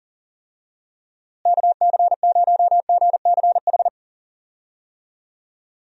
Morse code, heard as a keyed tone, about 37 wpm, 700 Hz. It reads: KC0GCH
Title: USA Call Signs - 6 Characters - Repeated 3x - 40wpm